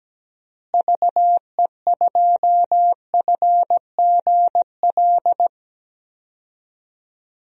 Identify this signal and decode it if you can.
Morse code, heard as a keyed tone, about 17 wpm, 700 Hz. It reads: VE2FGL
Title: DXCC-100 Call Signs - 6 Characters - Repeated 3x - 17wpm